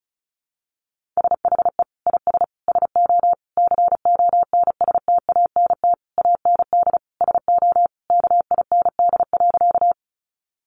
Morse code, heard as a keyed tone, about 34 wpm, 700 Hz. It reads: SHE IS SO CONSTANT AND SO KIND.
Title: Sentences from Top 1500 Words 35wpm